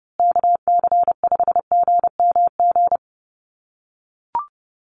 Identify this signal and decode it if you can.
Morse code, heard as a keyed tone, about 30 wpm, 700 Hz. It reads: KC5GMG
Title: USA Call Signs Repeated 3x 30wpm